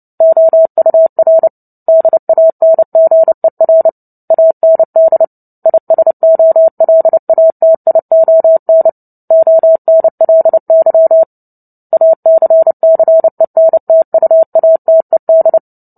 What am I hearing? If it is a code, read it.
Morse code, heard as a keyed tone, about 30 wpm, 650 Hz. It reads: OUR DANGER AND ISOLATION ONLY ACCENTUATED
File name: Book Two / 03-Chapter 3 - 29 WPM